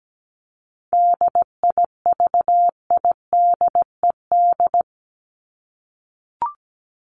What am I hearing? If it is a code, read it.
Morse code, heard as a keyed tone, about 17 wpm, 700 Hz. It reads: DIVIDED
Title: ED Suffix - Top 100 Words with Base Words 17wpm